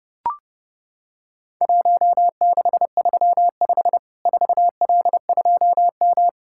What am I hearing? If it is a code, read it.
Morse code, heard as a keyed tone, about 30 wpm, 700 Hz. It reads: 1635 4L2M
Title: Worked All Europe DX Contest - QTC Exchange - 30wpm